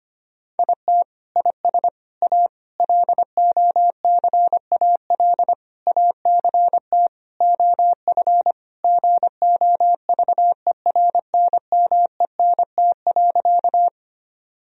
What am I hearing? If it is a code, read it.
Morse code, heard as a keyed tone, about 25 wpm, 700 Hz. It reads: IT IS A LOCAL ACT OF GOVERNMENT.
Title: Sentences from Top 1000 Words - Review Part 3 - 25wpm